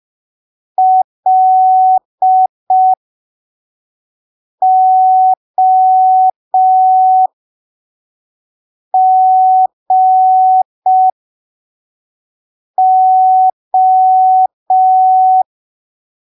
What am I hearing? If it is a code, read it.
Morse code, heard as a keyed tone, about 5 wpm, spaced as for 3 wpm, 750 Hz. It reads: LOGO
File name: arquivos / texto / texto 08 05ppm